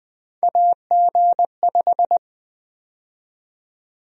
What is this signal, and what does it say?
Morse code, heard as a keyed tone, about 20 wpm, 700 Hz. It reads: AG5
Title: USA Call Sign Prefixes - Repeated 2x - 20wpm